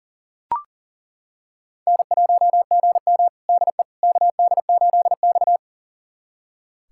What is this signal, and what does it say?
Morse code, heard as a keyed tone, about 40 wpm, 700 Hz. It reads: N1GM DE KD8X
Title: USA Callsign DE Callsign - 4 Characters - 40wpm